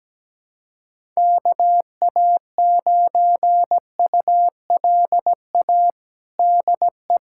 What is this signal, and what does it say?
Morse code, heard as a keyed tone, about 17 wpm, 700 Hz. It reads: KA9ULA DE
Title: USA Callsign DE Callsign - 6 Characters - 17wpm